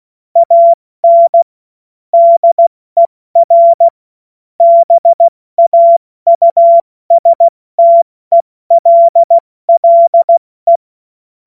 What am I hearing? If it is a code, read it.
Morse code, heard as a keyed tone, about 16 wpm, 680 Hz. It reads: AN DER BAUSTELLE
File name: Morsekurs / Witze / Witz 20 Februar 2024 70 F80